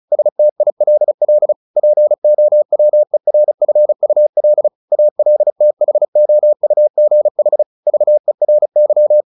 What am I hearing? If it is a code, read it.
Morse code, heard as a keyed tone, about 34 wpm, 600 Hz. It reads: STILL POWERFUL ALTHOUGH VERY
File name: the old man and the sea / the old man and the sea 1 35wpm